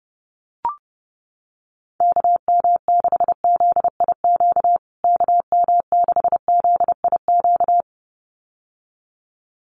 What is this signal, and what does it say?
Morse code, heard as a keyed tone, about 30 wpm, 700 Hz. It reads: KM6ZIQ KM6ZIQ
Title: USA Call Signs Repeated 2x 30wpm